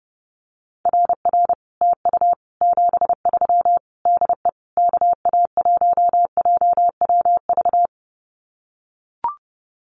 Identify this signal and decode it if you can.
Morse code, heard as a keyed tone, about 30 wpm, 700 Hz. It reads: RR TU 73 DE KA1JWV
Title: Common QSO Phrases Encore 30wpm